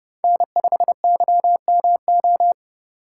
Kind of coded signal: Morse code

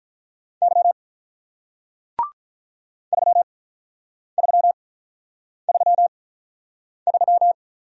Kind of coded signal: Morse code

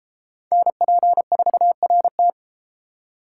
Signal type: Morse code

